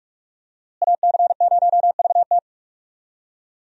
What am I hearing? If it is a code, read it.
Morse code, heard as a keyed tone, about 47 wpm, 700 Hz. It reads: AC0VT
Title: USA Call Signs - 5 Characters - Repeated 2x - 45wpm